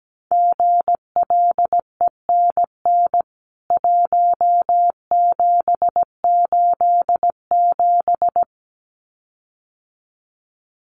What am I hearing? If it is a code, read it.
Morse code, heard as a keyed tone, about 17 wpm, 700 Hz. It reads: GLENN 1787
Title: CWT Exchange 17wpm